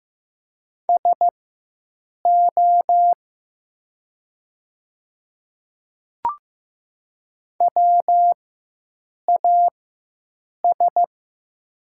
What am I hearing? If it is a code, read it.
Morse code, heard as a keyed tone, about 15 wpm, spaced as for 7 wpm, 700 Hz. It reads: SO WAS
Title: 4x Character Spacing - Top 100 Words - 15wpm